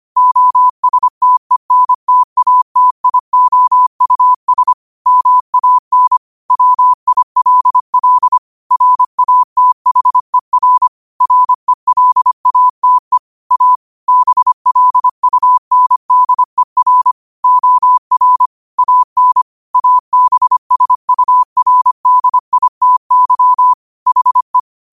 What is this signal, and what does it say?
Morse code, heard as a keyed tone, about 25 wpm, 1000 Hz. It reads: OSTENTATIOUS MAN WILL RATHER RELATE A BLUNDER OR AN ABSURDITY HE
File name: qotdfiles / 25wpm